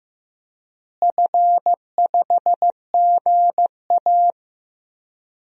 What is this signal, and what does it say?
Morse code, heard as a keyed tone, about 15 wpm, 700 Hz. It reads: F5GA
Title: DXCC-100 Call Signs - 4 Characters - Repeated 3x - 15wpm